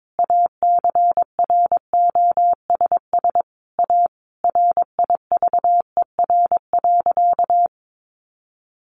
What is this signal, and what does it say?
Morse code, heard as a keyed tone, about 22 wpm, 700 Hz. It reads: ACROSS A RIVER.
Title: Sentences from Top 1200 Words 22wpm